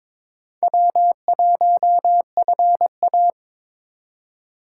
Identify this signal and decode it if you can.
Morse code, heard as a keyed tone, about 23 wpm, 700 Hz. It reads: W1FA
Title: USA Call Signs - 2x Character Spacing - 22wpm